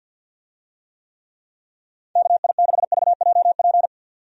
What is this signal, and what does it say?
Morse code, heard as a keyed tone, about 52 wpm, 700 Hz. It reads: KI6VJP
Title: USA Call Signs - 6 Characters - 50wpm